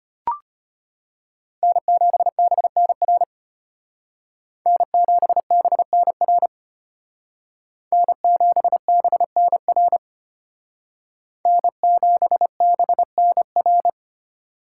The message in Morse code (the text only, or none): N7BNR N7BNR N7BNR N7BNR